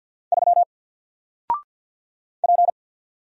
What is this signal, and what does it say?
Morse code, heard as a keyed tone, about 48 wpm, 700 Hz. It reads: 3 P
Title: Single Character (Letter:Number:Punctuation:Prosign) 50wpm